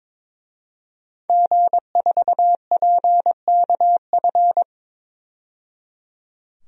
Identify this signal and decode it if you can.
Morse code, heard as a keyed tone, about 22 wpm, 700 Hz. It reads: G4PKF
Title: DXCC-100 Call Signs - 5 Characters - 22wpm